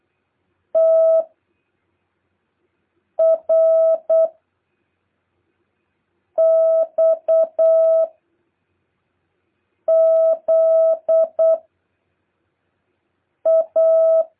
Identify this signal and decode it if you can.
Morse code, heard as a keyed tone, about 8 wpm, 650 Hz. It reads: T R X Z A
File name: jh8cbh / 2023cw